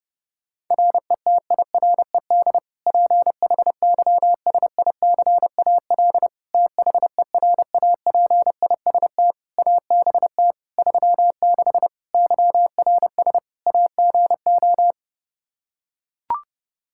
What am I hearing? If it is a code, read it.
Morse code, heard as a keyed tone, about 32 wpm, 700 Hz. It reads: RETIRED PHYSICAL THERAPIST ABT 36 YRS AGO
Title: Practice QSO - Exchange 3 - More Info - 30wpm